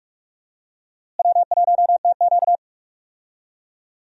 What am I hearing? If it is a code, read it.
Morse code, heard as a keyed tone, about 46 wpm, 700 Hz. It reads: W1TQ
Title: USA Call Signs - 4 Characters - Repeated 2x - 45wpm